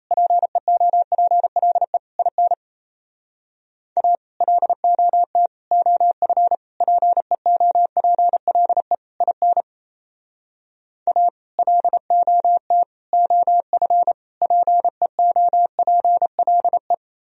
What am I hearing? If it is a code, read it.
Morse code, heard as a keyed tone, about 37 wpm, 700 Hz. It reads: PEOPLE IN A LOT OF PEOPLE IN A LOT OF PEOPLE